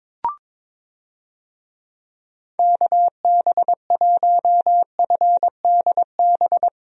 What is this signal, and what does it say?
Morse code, heard as a keyed tone, about 23 wpm, 700 Hz. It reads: KB1FDB
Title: USA Call Signs - 6 Characters - 22wpm